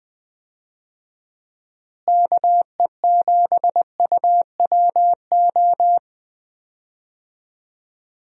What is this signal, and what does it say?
Morse code, heard as a keyed tone, about 20 wpm, 700 Hz. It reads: KE7UWO
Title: USA Call Signs - 6 Characters - Repeated 3x - 20wpm